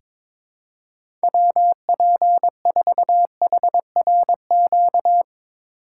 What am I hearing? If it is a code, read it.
Morse code, heard as a keyed tone, about 22 wpm, 700 Hz. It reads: WP4HRQ
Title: USA Call Signs - 6 Characters - 2x Character Spacing - 22wpm